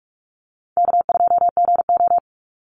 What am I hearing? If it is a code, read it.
Morse code, heard as a keyed tone, about 44 wpm, 700 Hz. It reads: K2GO
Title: Easy Call Signs Repeated 3x - 45wpm